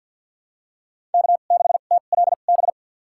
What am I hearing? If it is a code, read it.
Morse code, heard as a keyed tone, about 53 wpm, spaced as for 37 wpm, 700 Hz. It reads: K6TLB
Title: USA Call Signs - 5 Characters - 2x Character Spacing - 50wpm